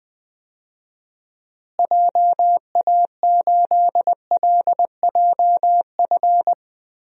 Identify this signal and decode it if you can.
Morse code, heard as a keyed tone, about 20 wpm, 700 Hz. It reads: JA8LJF
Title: DXCC-100 Call Signs - 6 Characters - Repeated 3x - 20wpm